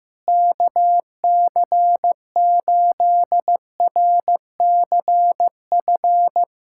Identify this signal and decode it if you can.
Morse code, heard as a keyed tone, about 15 wpm, 700 Hz. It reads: KC8RCF
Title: USA Callsign DE Callsign - 6 Characters - 15wpm